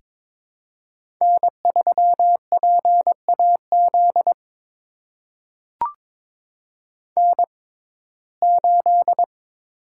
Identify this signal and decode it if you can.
Morse code, heard as a keyed tone, about 22 wpm, spaced as for 16 wpm, 700 Hz. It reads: N3PAZ N8
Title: USA Call Signs - 5 Characters - 6x Character Spacing - 22wpm